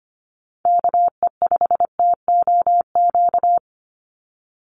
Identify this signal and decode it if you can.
Morse code, heard as a keyed tone, about 25 wpm, 700 Hz. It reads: KE5TOQ
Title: USA Call Signs 25wpm